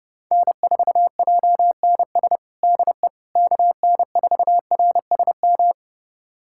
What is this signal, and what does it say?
Morse code, heard as a keyed tone, about 30 wpm, 700 Hz. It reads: N4JNS DE KN4RSM
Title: USA Callsign DE Callsign 30wpm